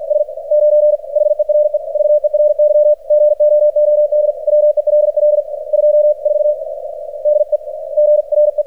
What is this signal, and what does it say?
Morse code, heard as a keyed tone, about 48 wpm, spaced as for 41 wpm, 600 Hz, with radio noise. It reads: S9 HERE VERY GOOD WELL 24 DEGRE
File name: archived / CRSnet